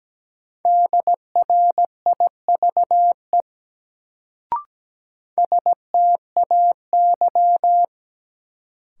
Morse code, heard as a keyed tone, about 17 wpm, 700 Hz. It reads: DRIVE STAY
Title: Sentences from Top 500 Words - New Word Drill - 17wpm